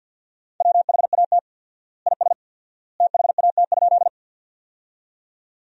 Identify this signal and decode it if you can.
Morse code, heard as a keyed tone, about 51 wpm, 700 Hz. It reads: WHAT IS THAT?
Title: 4x Word-Spacing - Sentences from Top 100 Words - 50wpm